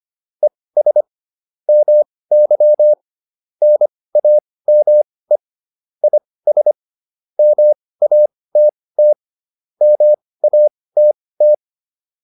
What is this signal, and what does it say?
Morse code, heard as a keyed tone, about 25 wpm, spaced as for 18 wpm, 600 Hz. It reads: ES MY NAME IS MATT MATT